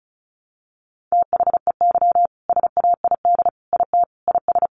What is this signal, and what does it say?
Morse code, heard as a keyed tone, about 34 wpm, 700 Hz. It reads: THEY SAID IT IS
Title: Sentences from Top 1100 Words 35wpm